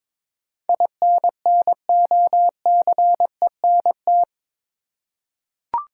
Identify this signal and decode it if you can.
Morse code, heard as a keyed tone, about 22 wpm, 700 Hz. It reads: INNOCENT
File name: practice / 22wpm / Eight Letter Words - 22wpm